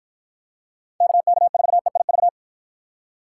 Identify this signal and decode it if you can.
Morse code, heard as a keyed tone, about 53 wpm, 700 Hz. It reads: KK4EIV